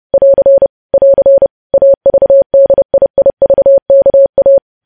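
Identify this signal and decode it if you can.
Morse code, heard as a keyed tone, about 30 wpm, 550 Hz. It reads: <AR> <AR> AVDIIVKA
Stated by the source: Morse news headlines sent at 30.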